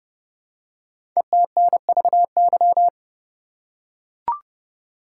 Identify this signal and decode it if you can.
Morse code, heard as a keyed tone, about 30 wpm, 700 Hz. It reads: ETNVY
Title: Code Groups - 5 Letters - 30wpm